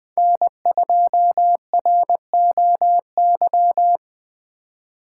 Morse code, heard as a keyed tone, about 21 wpm, 700 Hz. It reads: N2ROY